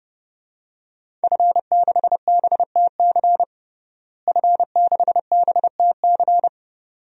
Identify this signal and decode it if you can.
Morse code, heard as a keyed tone, about 32 wpm, 700 Hz. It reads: F6BTC F6BTC
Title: DXCC-100 Call Signs - 5 Characters - Repeated 2x - 30wpm